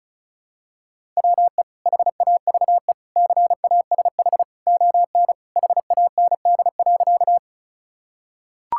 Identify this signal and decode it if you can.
Morse code, heard as a keyed tone, about 34 wpm, 700 Hz. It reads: WE HAVE CASH ON HAND.
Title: Sentences from Top 1000 Words - Review Part 5 - 35wpm